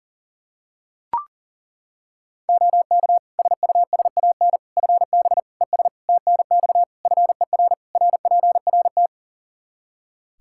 Morse code, heard as a keyed tone, about 40 wpm, 700 Hz. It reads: OK SUSAN FB ES TNX FER RPRT
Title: Practice QSO - Exchange 2 - Info - 40wpm